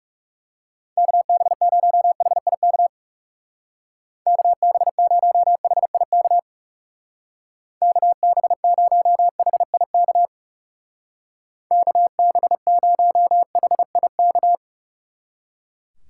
Morse code, heard as a keyed tone, about 36 wpm, 700 Hz. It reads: KB0HIK KB0HIK KB0HIK KB0HIK
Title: Speed Racing - USA 2x3 Call Signs 30wpm